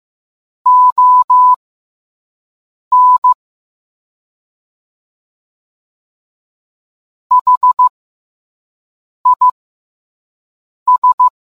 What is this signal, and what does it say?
Morse code, heard as a keyed tone, about 15 wpm, spaced as for 5 wpm, 1000 Hz. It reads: ON HIS